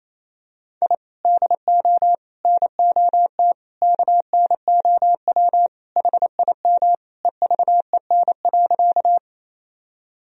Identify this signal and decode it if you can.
Morse code, heard as a keyed tone, about 28 wpm, 700 Hz. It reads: I DO NOT KNOW HIM EVEN.
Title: Sentences from Top 1000 Words - Review Part 1 - 28wpm